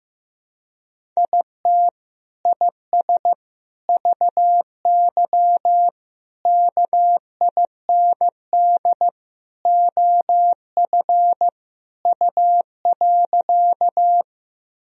Morse code, heard as a keyed tone, about 16 wpm, 700 Hz. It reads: IT IS VY KIND OF U.
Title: Sentences from Top 200 Words and Abbreviations 15wpm